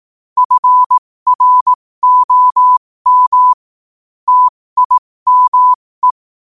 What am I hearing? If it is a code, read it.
Morse code, heard as a keyed tone, about 18 wpm, spaced as for 15 wpm, 1000 Hz. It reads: FROM TIME
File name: qotdfiles / 15wpm